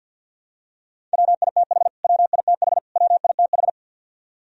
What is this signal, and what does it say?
Morse code, heard as a keyed tone, about 51 wpm, 700 Hz. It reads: WITH WITH WITH
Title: Top 100 Words Repeated 50wpm